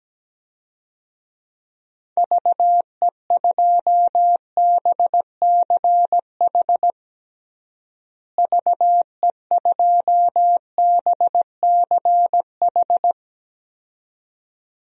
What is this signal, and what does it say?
Morse code, heard as a keyed tone, about 17 wpm, 700 Hz. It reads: VE2BCH VE2BCH
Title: DXCC-100 Call Signs - 6 Characters - Repeated 2x - 17wpm